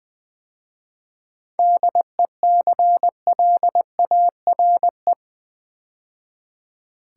Morse code, heard as a keyed tone, about 20 wpm, 690 Hz. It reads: DECLARE
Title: 7 Letter Words - First 3 Letters Given - 20wpm